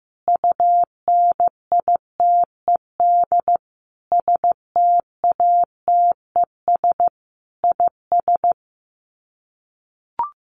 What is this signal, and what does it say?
Morse code, heard as a keyed tone, about 14 wpm, 700 Hz. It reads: UNITED STATES IS
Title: Sets of 5 Words 15wpm